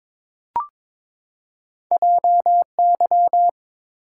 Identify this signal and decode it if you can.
Morse code, heard as a keyed tone, about 22 wpm, 690 Hz. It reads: JY
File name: practice / 22wpm / Two Letter-Number Combinations - 22wpm